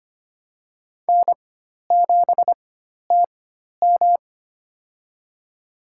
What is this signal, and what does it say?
Morse code, heard as a keyed tone, about 25 wpm, spaced as for 12 wpm, 700 Hz. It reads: N7TM
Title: USA Call Signs - 4x Character Spacing - 25wpm